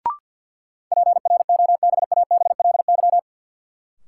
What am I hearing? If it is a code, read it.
Morse code, heard as a keyed tone, about 48 wpm, 700 Hz. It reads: PROBABLY
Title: Sentences from Top 700 Words - New Word Drill - 50wpm